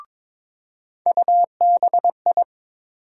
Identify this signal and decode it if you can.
Morse code, heard as a keyed tone, about 22 wpm, 700 Hz. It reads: UBI
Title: USA Call Sign Postfixes - 3 Letters 22wpm